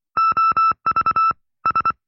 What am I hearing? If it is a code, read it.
Morse code, heard as a keyed tone, about 24 wpm, 1310 Hz. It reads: OV S
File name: FreeFiles / CW-practice / 20WPM-side2